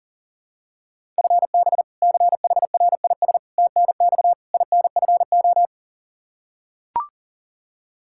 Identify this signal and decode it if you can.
Morse code, heard as a keyed tone, about 40 wpm, 700 Hz. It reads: FB CHRIS TNX INFO
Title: Practice QSO - Exchange 3 - More Info - 40wpm